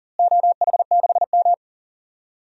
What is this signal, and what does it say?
Morse code, heard as a keyed tone, about 39 wpm, 700 Hz. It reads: OH6M
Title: DXCC-100 Call Signs - 4 Characters - Repeated 2x - 40wpm